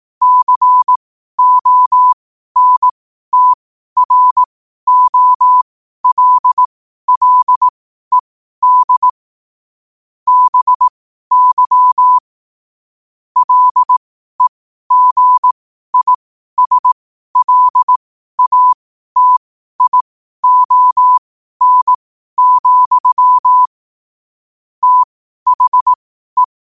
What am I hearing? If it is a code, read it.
Morse code, heard as a keyed tone, about 18 wpm, spaced as for 12 wpm, 1000 Hz. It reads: CONTROLLED BY LEGISLATION, THE
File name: qotdfiles / 12wpm